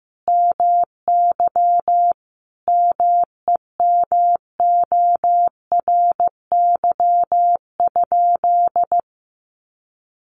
Morse code, heard as a keyed tone, about 15 wpm, 700 Hz. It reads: MY MEMORY?
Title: Sentences from Top 1300 Words 15wpm